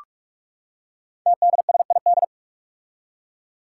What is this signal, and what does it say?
Morse code, heard as a keyed tone, about 44 wpm, 700 Hz. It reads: TDSID
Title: Five Challenging Letter-Number Combinations 45wpm